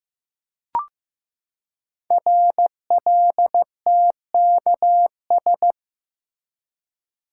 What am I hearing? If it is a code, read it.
Morse code, heard as a keyed tone, about 15 wpm, 700 Hz. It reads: RLTKS
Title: Code Groups - 5 Letters - 15wpm